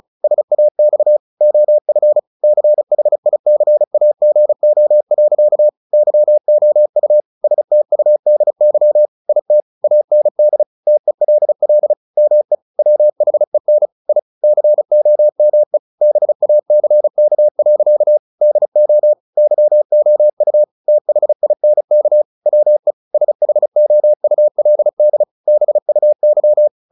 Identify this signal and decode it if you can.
Morse code, heard as a keyed tone, about 34 wpm, 600 Hz. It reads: SAX OF CHICAGO. YOU STUDY IT AND TELL ME WHEN I COME BACK. DO YOU THINK WE SHOULD BUY
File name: the old man and the sea / the old man and the sea 1 35wpm